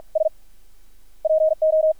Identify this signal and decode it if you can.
Morse code, heard as a keyed tone, about 53 wpm, 630 Hz. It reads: S PO
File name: archived / CRSnet